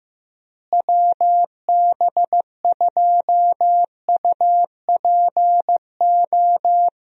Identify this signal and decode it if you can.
Morse code, heard as a keyed tone, about 15 wpm, 700 Hz. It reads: WB2UPO
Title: USA Call Signs - 6 Characters - Repeated 3x - 15wpm